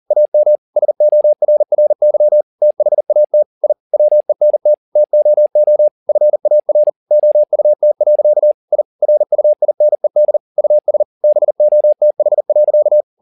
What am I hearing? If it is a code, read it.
Morse code, heard as a keyed tone, about 40 wpm, 600 Hz. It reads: AM SORRY THAT I WENT TOO FAR OUT. I RUINED US BOTH.
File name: the old man and the sea 2 40wpm